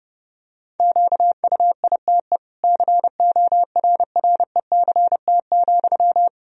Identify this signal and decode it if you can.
Morse code, heard as a keyed tone, about 30 wpm, 700 Hz. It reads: QUITE CORRECT,
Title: Sentences from Top 1000 Words - Review Part 2 - 30wpm